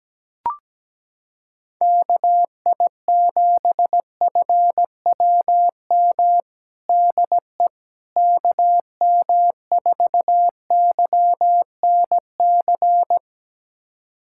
Morse code, heard as a keyed tone, about 17 wpm, 700 Hz. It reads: KI7FWM DE KM4YNC
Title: USA Callsign DE Callsign - 6 Characters - 17wpm